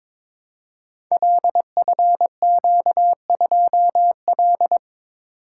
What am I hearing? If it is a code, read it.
Morse code, heard as a keyed tone, about 22 wpm, 700 Hz. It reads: LFQ2L
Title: Code Groups - 5 Letter:Numbers - 22wpm